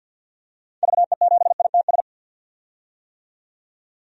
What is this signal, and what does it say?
Morse code, heard as a keyed tone, about 47 wpm, 700 Hz. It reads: VE7ITS
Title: DXCC-100 Call Signs - 6 Characters - 50wpm